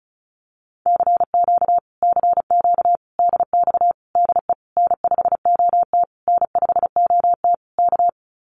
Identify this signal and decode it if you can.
Morse code, heard as a keyed tone, about 34 wpm, 700 Hz. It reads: CQ CQ DX DE N5OT N5OT K
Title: Common QSO Phrases Encore 35wpm